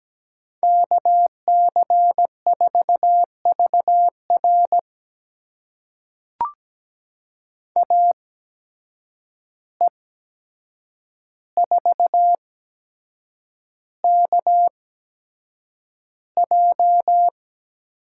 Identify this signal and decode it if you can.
Morse code, heard as a keyed tone, about 17 wpm, spaced as for 7 wpm, 700 Hz. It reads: KC4VR AE4KJ
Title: USA Call Signs - 5 Characters - 8x Character Spacing - 17wpm